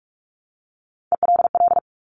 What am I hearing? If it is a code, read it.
Morse code, heard as a keyed tone, about 46 wpm, 700 Hz. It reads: ELL
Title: Top Three Letter Combinations in Words 45wpm